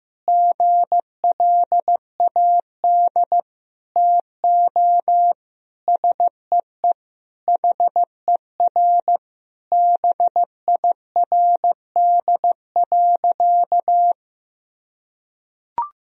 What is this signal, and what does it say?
Morse code, heard as a keyed tone, about 16 wpm, 700 Hz. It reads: GLAD TO SEE HER BIRD.
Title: Sentences from Top 1000 Words - Review Part 2 - 15wpm